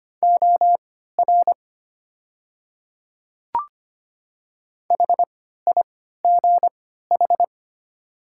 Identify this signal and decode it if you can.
Morse code, heard as a keyed tone, about 25 wpm, spaced as for 14 wpm, 690 Hz. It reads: OR HIGH
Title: 3x Character Spacing - Top 200 Words - 25wpm